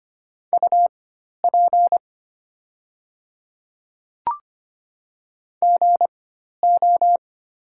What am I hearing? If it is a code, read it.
Morse code, heard as a keyed tone, about 25 wpm, spaced as for 12 wpm, 700 Hz. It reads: UP GO
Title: 4x Character Spacing - Top 200 Words - 25wpm